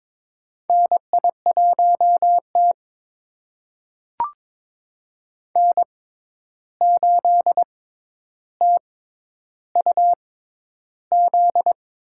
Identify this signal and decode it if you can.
Morse code, heard as a keyed tone, about 22 wpm, spaced as for 11 wpm, 700 Hz. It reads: NI1T N8TUZ
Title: USA Call Signs - 6x Character Spacing - 22wpm